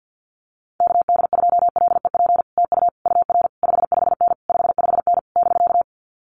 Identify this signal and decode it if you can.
Morse code, heard as a keyed tone, about 50 wpm, 700 Hz. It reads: KD2LEP TU UR 55N 55N <BK>
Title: Common QSO Phrases Encore 50wpm